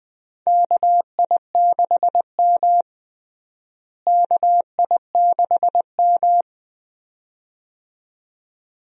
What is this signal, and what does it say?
Morse code, heard as a keyed tone, about 20 wpm, 700 Hz. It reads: KI6M KI6M
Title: USA Call Signs - 4 Characters - Repeated 2x - 20wpm